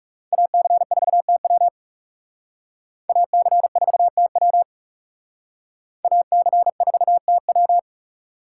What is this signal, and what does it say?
Morse code, heard as a keyed tone, about 39 wpm, 700 Hz. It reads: AC4TW AC4TW AC4TW